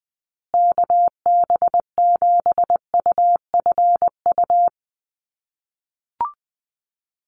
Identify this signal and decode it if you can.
Morse code, heard as a keyed tone, about 20 wpm, 700 Hz. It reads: KB7UFU
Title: USA Call Signs Repeated 3x 20wpm